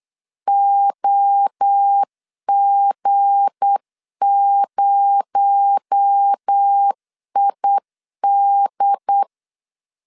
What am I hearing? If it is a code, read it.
Morse code, heard as a keyed tone, about 8 wpm, 800 Hz. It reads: OG0ID
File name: LEZIONI / 7 GF9Z / 7 G 8wpm 3118